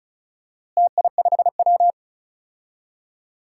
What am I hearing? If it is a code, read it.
Morse code, heard as a keyed tone, about 34 wpm, 700 Hz. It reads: TI5W
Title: DXCC-100 Call Signs - 4 Characters - Repeated 2x - 35wpm